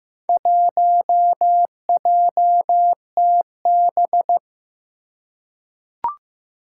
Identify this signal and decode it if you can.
Morse code, heard as a keyed tone, about 15 wpm, 700 Hz. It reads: 1JTB
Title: Four Challenging Letter-Number Combinations 15wpm